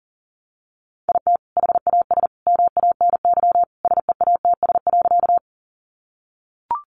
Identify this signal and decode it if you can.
Morse code, heard as a keyed tone, about 40 wpm, 700 Hz. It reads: IT HAS MANY SEATS.
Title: Additional Easy Phrases 40wpm